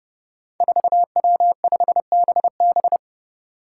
Morse code, heard as a keyed tone, about 29 wpm, 700 Hz. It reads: 4W5BB
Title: Five Challenging Letter-Number Combinations 30wpm